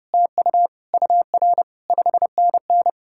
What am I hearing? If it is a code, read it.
Morse code, heard as a keyed tone, about 40 wpm, spaced as for 35 wpm, 700 Hz. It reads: TU UR 5NN
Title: Speed-Racing Common QSO Phrases - 30wpm